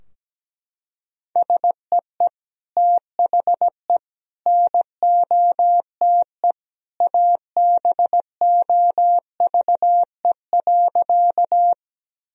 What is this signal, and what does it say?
Morse code, heard as a keyed tone, about 17 wpm, 700 Hz. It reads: SEE THE NOTE ABOVE.
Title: Sentences from Top 1000 Words - Review Part 2 - 17wpm